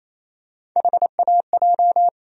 Morse code, heard as a keyed tone, about 28 wpm, 700 Hz. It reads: HAJ
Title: USA Call Sign Postfixes - 3 Letters 28wpm